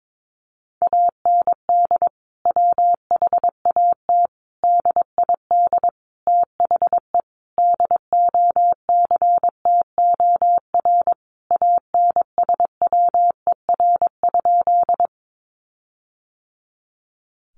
Morse code, heard as a keyed tone, about 22 wpm, 700 Hz. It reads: AND WHAT DID THE DOCTOR ANSWER?
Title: Sentences from Top 1500 Words 22wpm